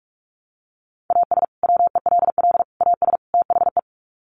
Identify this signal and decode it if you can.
Morse code, heard as a keyed tone, about 44 wpm, 700 Hz. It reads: AS WELL AS THE
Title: Sets of 4 Words 45wpm